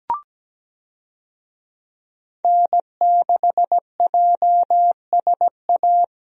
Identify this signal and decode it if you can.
Morse code, heard as a keyed tone, about 17 wpm, 700 Hz. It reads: N6JSA
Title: USA Call Signs - 5 Characters - Repeated 3x - 17wpm